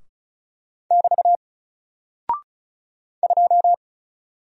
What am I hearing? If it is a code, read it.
Morse code, heard as a keyed tone, about 36 wpm, 700 Hz. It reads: <BT> 2
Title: Single Character (Letter:Number:Punctuation:Prosign) Alt 35wpm